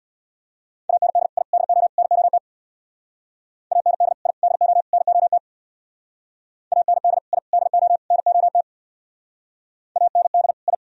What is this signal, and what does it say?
Morse code, heard as a keyed tone, about 59 wpm, 700 Hz. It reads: AND I DO NOT AND I DO NOT AND I DO NOT AND I
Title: Speed-Racing - Sets of 4 Words 50wpm